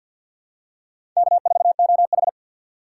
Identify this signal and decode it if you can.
Morse code, heard as a keyed tone, about 54 wpm, 700 Hz. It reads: K4OH
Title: USA Call Signs - 4 Characters - 3x Character Spacing - 50wpm